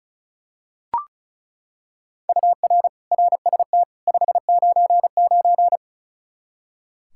Morse code, heard as a keyed tone, about 36 wpm, 700 Hz. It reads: UR RST 599
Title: Practice QSO - Exchange 1 - Beginning - 35wpm